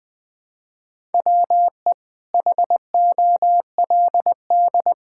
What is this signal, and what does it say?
Morse code, heard as a keyed tone, about 20 wpm, 700 Hz. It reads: WE HOLD
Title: Sentences with 9-Word Phrases - Spoken Beginning - 20wpm